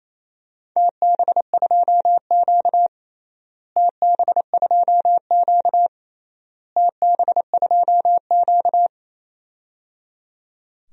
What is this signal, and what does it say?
Morse code, heard as a keyed tone, about 28 wpm, 700 Hz. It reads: TB2Q TB2Q TB2Q
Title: DXCC-100 Call Signs - 4 Characters - Repeated 3x - 28wpm